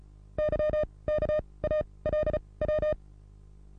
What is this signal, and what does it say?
Morse code, heard as a keyed tone, about 35 wpm, spaced as for 23 wpm, 610 Hz. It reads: YKALW
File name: repos / Morsekurs / DJ4UF Morsen27